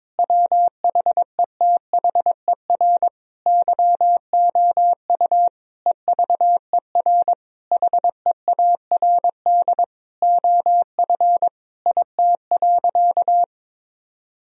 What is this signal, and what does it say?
Morse code, heard as a keyed tone, about 22 wpm, 700 Hz. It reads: WHETHER YOU EVER HEARD OF IT.
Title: Sentences from Top 1000 Words - Review Part 3 - 22wpm